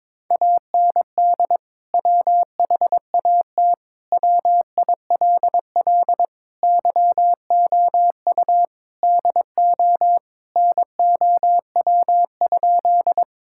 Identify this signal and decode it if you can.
Morse code, heard as a keyed tone, about 21 wpm, 700 Hz. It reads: AND WHAT WILL YOU DO NOW?
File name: practice / 22wpm / Sentences with 2-Word Phrases - 22wpm